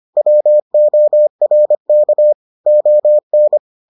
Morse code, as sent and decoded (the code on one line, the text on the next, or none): .-- --- .-. -.- / --- -.
WORK ON